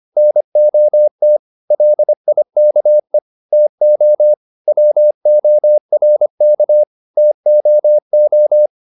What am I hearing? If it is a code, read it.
Morse code, heard as a keyed tone, about 25 wpm, 600 Hz. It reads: NOT LIKE TO WORK TOO
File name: the old man and the sea part 1 25wpm